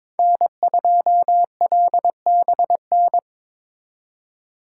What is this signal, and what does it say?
Morse code, heard as a keyed tone, about 26 wpm, 700 Hz. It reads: N2LBN
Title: Speed Racing - USA 1x3 Call Signs 22wpm